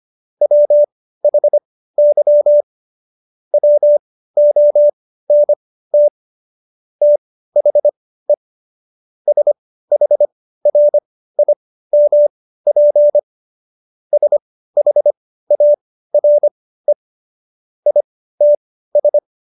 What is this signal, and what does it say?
Morse code, heard as a keyed tone, about 25 wpm, spaced as for 15 wpm, 590 Hz. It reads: WHY WONT THE SHRIMP SHARE ITS
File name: Jokes-25-char-15-eff